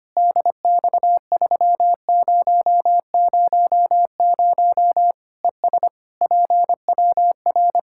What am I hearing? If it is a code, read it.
Morse code, heard as a keyed tone, about 25 wpm, 700 Hz. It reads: DX3000 ES PWR
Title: Practice QSO - Exchange 2 - Info - 25wpm